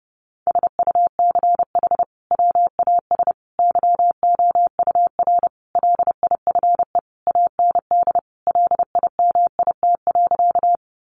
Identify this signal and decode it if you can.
Morse code, heard as a keyed tone, about 30 wpm, 700 Hz. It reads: SUCH WAS YOUR LIFE AND LIMIT.
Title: Sentences from Top 1500 Words 30wpm